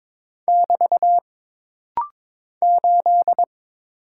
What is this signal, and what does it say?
Morse code, heard as a keyed tone, about 22 wpm, 700 Hz. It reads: <BT> 8
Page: This rapid-fire timing is geared to push your ICR (Instant Character Recognition) skills.